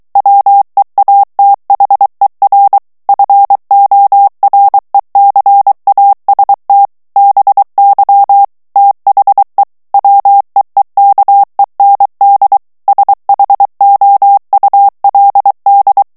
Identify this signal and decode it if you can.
Morse code, heard as a keyed tone, about 23 wpm, 800 Hz. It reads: WEATHER FORECAST BY THE WEEKEND SHOULD
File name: morse mp3 / ANGLAIS texte clair 2 vitesse 22 mots minute